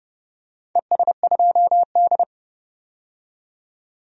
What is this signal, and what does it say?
Morse code, heard as a keyed tone, about 30 wpm, 700 Hz. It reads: ES2D
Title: DXCC-100 Call Signs - 4 Characters - 30wpm